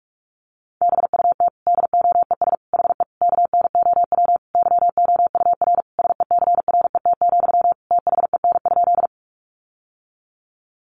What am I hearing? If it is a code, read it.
Morse code, heard as a keyed tone, about 44 wpm, 700 Hz. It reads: BUT DOES HE KNOW YOUR SECRET, THEN?
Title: Sentences from Top 1200 Words 45wpm